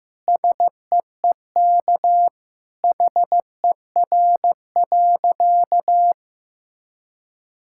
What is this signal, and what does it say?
Morse code, heard as a keyed tone, about 15 wpm, 700 Hz. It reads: SEEK HER.
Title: Sentences from Top 1600 Words 15wpm